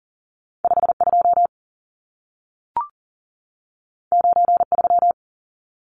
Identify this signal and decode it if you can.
Morse code, heard as a keyed tone, about 40 wpm, 700 Hz. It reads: 52 93
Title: Numbers - 2-Digit 40wpm